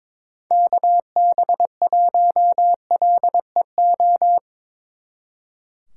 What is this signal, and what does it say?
Morse code, heard as a keyed tone, about 24 wpm, 700 Hz. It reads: KB1LEO
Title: Speed Racing - USA 2x3 Call Signs 22wpm